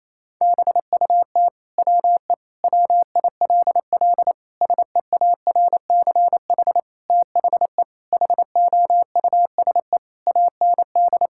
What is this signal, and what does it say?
Morse code, heard as a keyed tone, about 28 wpm, 700 Hz. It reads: BUT WE WILL SEARCH THE HOUSE AND
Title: Sentences from Top 1000 Words - Review Part 5 - 28wpm